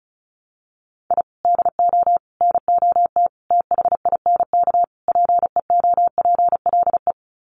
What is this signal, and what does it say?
Morse code, heard as a keyed tone, about 34 wpm, 700 Hz. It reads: I DO NOT THINK PEOPLE
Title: Sets of 5 Words Encore2 35wpm